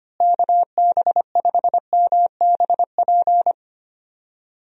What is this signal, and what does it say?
Morse code, heard as a keyed tone, about 25 wpm, 700 Hz. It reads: KB5MBP
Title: USA Call Signs - 6 Characters - Repeated 2x - 25wpm